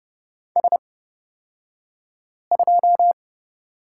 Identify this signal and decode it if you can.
Morse code, heard as a keyed tone, about 31 wpm, spaced as for 5 wpm, 700 Hz. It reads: S2
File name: Lesson 37d - Letter x - Two Characters 30wpm